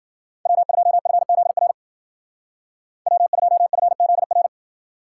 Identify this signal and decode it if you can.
Morse code, heard as a keyed tone, about 52 wpm, 700 Hz. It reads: W2FZR W2FZR